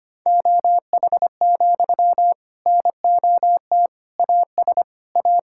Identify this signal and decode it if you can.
Morse code, heard as a keyed tone, about 25 wpm, 700 Hz. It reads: OH, NOT AS A
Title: Sentences from Top 1000 Words - Review Part 5 - 25wpm